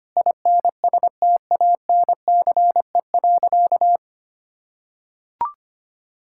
Morse code, heard as a keyed tone, about 25 wpm, 700 Hz. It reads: INSTANCE.
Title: Sentences from Top 1000 Words - Review Part 5 - 25wpm